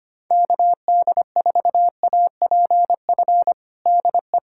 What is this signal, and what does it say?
Morse code, heard as a keyed tone, about 25 wpm, 700 Hz. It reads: KD4APF DE
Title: USA Callsign DE Callsign - 6 Characters - 25wpm